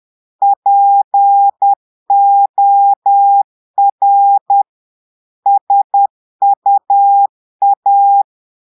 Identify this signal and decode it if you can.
Morse code, heard as a keyed tone, about 10 wpm, 800 Hz. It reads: POR SUA
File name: Portugais - texte en clair 1 - vitesse - 10 mots minute